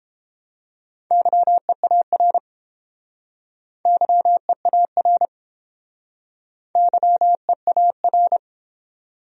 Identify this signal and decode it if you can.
Morse code, heard as a keyed tone, about 30 wpm, 700 Hz. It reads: YEAR YEAR YEAR